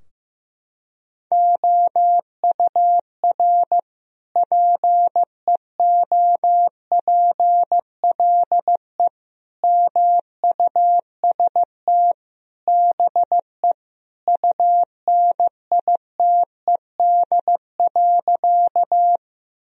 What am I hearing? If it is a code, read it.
Morse code, heard as a keyed tone, about 15 wpm, 700 Hz. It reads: OUR PEOPLE MUST BE UNITED.
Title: Sentences from Top 1000 Words - Review Part 4 - 15wpm